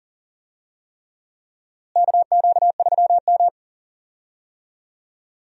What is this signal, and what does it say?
Morse code, heard as a keyed tone, about 40 wpm, 700 Hz. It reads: KQ3M
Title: USA Call Signs - 4 Characters - Repeated 3x - 40wpm